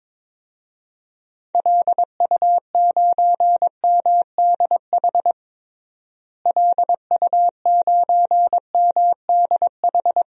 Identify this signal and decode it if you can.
Morse code, heard as a keyed tone, about 22 wpm, 700 Hz. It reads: LU9MDH LU9MDH
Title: DXCC-100 Call Signs - 6 Characters - Repeated 2x - 22wpm